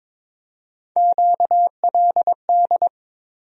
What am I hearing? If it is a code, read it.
Morse code, heard as a keyed tone, about 22 wpm, 700 Hz. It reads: QLD
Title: USA Call Sign Postfixes - 3 Letters 22wpm